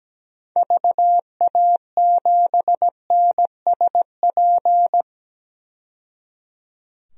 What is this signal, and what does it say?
Morse code, heard as a keyed tone, about 17 wpm, 700 Hz. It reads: VA7NSP
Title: DXCC-100 Call Signs - 6 Characters - 17wpm